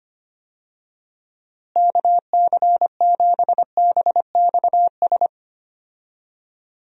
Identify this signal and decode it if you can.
Morse code, heard as a keyed tone, about 25 wpm, 700 Hz. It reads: KC7BXS
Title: USA Call Signs - 6 Characters - Repeated 3x - 25wpm